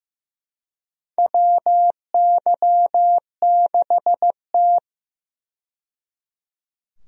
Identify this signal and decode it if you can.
Morse code, heard as a keyed tone, about 15 wpm, 700 Hz. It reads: WY6T
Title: USA Call Signs - 4 Characters - 15wpm